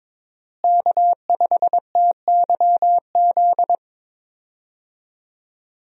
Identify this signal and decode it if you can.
Morse code, heard as a keyed tone, about 22 wpm, 700 Hz. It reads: K5TYZ
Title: USA Call Signs - 5 Characters - Repeated 3x - 22wpm